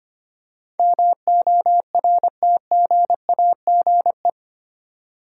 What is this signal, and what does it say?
Morse code, heard as a keyed tone, about 25 wpm, 700 Hz. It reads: MORTGAGE
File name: 8 Letter Words - First 3 Letters Given - 25wpm